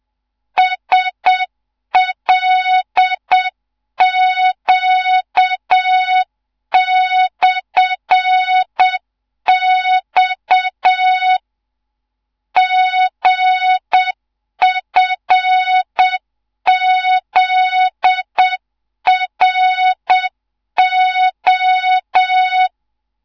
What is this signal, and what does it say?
Morse code, heard as a keyed tone, about 7 wpm, 3700 Hz, with noise. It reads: SLQ/X GFZRO
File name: Morze-HA2MN / hang / 14 280 HA2MN